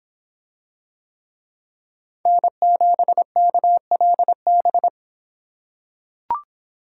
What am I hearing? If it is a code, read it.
Morse code, heard as a keyed tone, about 26 wpm, 700 Hz. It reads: N7KLB